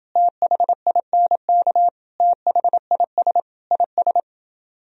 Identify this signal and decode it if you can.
Morse code, heard as a keyed tone, about 29 wpm, 700 Hz. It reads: THINK THIS IS